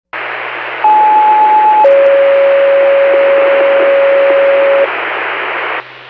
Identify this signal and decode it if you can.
Two-tone page, heard as a one-second tone, then a three-second tone, about 860 then 550 Hz, with noise. It is heard from 0.8 to 4.9 s.